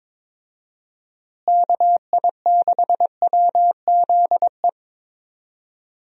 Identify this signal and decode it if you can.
Morse code, heard as a keyed tone, about 21 wpm, 700 Hz. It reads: KI6WZE
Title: USA Call Signs - 6 Characters - Repeated 2x - 22wpm